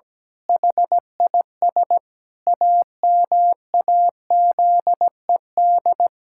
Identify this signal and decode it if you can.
Morse code, heard as a keyed tone, about 17 wpm, 700 Hz. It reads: HIS AMAZED
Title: Pangram Challenge v2 - 17wpm